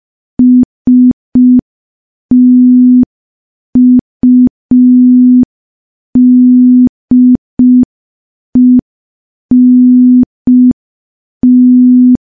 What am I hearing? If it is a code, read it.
Morse code, heard as a keyed tone, about 5 wpm, 250 Hz. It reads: STUDENT
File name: morse / morse-training-05WPM